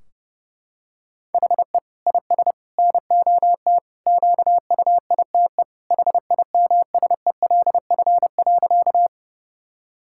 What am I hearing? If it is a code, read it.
Morse code, heard as a keyed tone, about 32 wpm, 700 Hz. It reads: HE IS NOT QUITE HIMSELF.
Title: Sentences from Top 1600 Words 30wpm